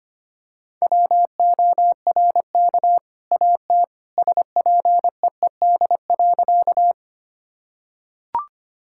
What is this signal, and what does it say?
Morse code, heard as a keyed tone, about 25 wpm, 700 Hz. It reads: WORK AT SPEED.
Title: Sentences from Top 1000 Words - Review Part 5 - 25wpm